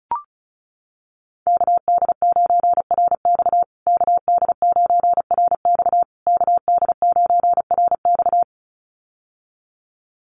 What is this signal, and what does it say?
Morse code, heard as a keyed tone, about 34 wpm, 700 Hz. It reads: KD9RX KD9RX KD9RX
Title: USA Call Signs Repeated 3x 35wpm